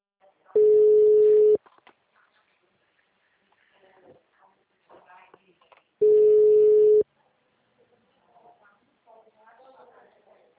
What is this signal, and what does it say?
Ringback tone (Europe, 425 Hz), with speech.